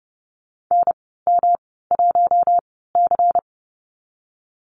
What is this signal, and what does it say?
Morse code, heard as a keyed tone, about 32 wpm, spaced as for 17 wpm, 700 Hz. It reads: NM1C
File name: Lesson 19g - Letter w - Call Signs 30wpm